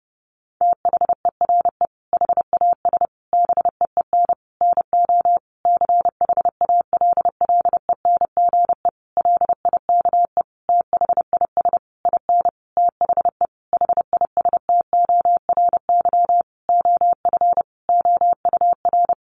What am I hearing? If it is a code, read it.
Morse code, heard as a keyed tone, about 30 wpm, 700 Hz. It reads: THERE HAS BEEN NO CHALLENGE LIKE THIS IN THE HISTORY OF OUR
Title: Sentences from Top 1500 Words 30wpm